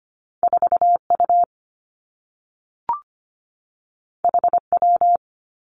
Morse code, heard as a keyed tone, about 25 wpm, 700 Hz. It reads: 4 U H W
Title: Two Challenging Letter-Number Combinations 25wpm